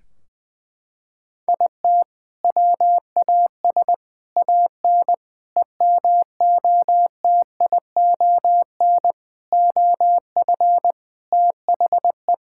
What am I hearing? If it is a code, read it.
Morse code, heard as a keyed tone, about 21 wpm, 700 Hz. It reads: IT WAS AN EMOTION OF THE
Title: Sentences from Top 1600 Words 20wpm